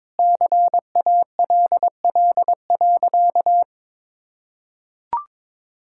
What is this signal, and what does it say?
Morse code, heard as a keyed tone, about 22 wpm, 700 Hz. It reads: CALL.
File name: practice / 22wpm / Sentences with 3-Word Phrases - 22wpm